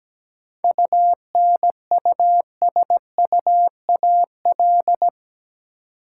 Morse code, heard as a keyed tone, about 17 wpm, 700 Hz. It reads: UNUSUAL
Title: Seven Letter Words - 17wpm